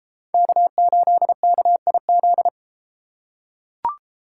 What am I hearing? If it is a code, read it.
Morse code, heard as a keyed tone, about 32 wpm, 700 Hz. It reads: K8KIZ